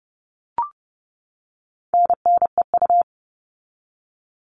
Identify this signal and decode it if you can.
Morse code, heard as a keyed tone, about 30 wpm, 700 Hz. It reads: NNEU
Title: All Cut Numbers - 4-Digit 30wpm